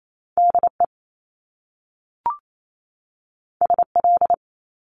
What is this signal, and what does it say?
Morse code, heard as a keyed tone, about 28 wpm, 700 Hz. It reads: DE SL
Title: Top Two Letter Combinations in Words 28wpm